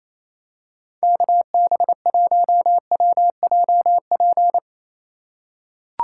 Morse code, heard as a keyed tone, about 28 wpm, 690 Hz. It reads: KB1WJP
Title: USA Call Signs - 2x Character Spacing - 28wpm